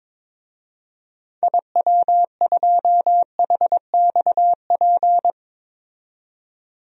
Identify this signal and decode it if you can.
Morse code, heard as a keyed tone, about 22 wpm, 700 Hz. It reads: IW2HXP
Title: DXCC-100 Call Signs - 6 Characters - 22wpm